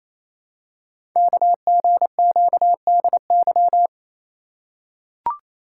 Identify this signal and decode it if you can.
Morse code, heard as a keyed tone, about 27 wpm, 700 Hz. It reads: KGQDY
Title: Code Groups - 5 Letters - 28wpm